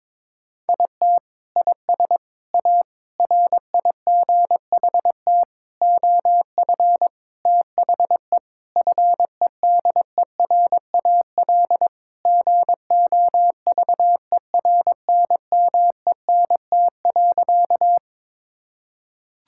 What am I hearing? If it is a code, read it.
Morse code, heard as a keyed tone, about 22 wpm, 700 Hz. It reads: IT IS A RIGHT OF THE FEDERAL GOVERNMENT.
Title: Sentences from Top 1000 Words - Review Part 5 - 22wpm